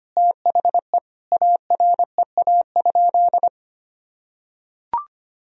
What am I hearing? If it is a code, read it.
Morse code, heard as a keyed tone, about 25 wpm, 700 Hz. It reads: THE AREA?
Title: Sentences from Top 1000 Words - Review Part 2 - 25wpm